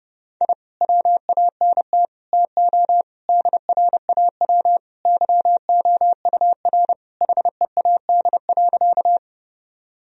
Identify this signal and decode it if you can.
Morse code, heard as a keyed tone, about 32 wpm, 700 Hz. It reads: I WANT TO DRAW YOUR HEAD.
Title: Sentences from Top 1000 Words - Review Part 1 - 30wpm